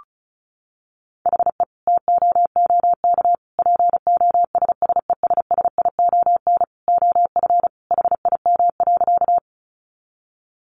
Morse code, heard as a keyed tone, about 34 wpm, 700 Hz. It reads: HE TOOK POSSESSION OF HIM.
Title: Sentences from Top 1400 Words 35wpm